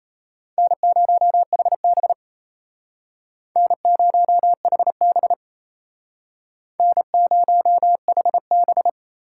Morse code, heard as a keyed tone, about 34 wpm, 700 Hz. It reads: N0HB N0HB N0HB
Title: Speed Racing - USA 1x2 Call Signs 28wpm